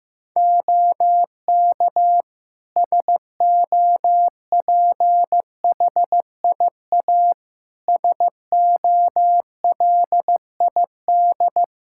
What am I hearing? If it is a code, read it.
Morse code, heard as a keyed tone, about 16 wpm, 700 Hz. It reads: OK SOPHIA SOLID
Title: Practice QSO - Exchange 3 - More Info - 15wpm